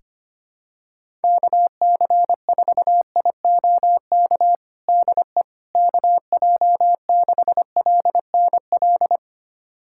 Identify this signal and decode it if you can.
Morse code, heard as a keyed tone, about 25 wpm, 700 Hz. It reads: KC4IOK DE KJ6LNL
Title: USA Callsign DE Callsign - 6 Characters - 25wpm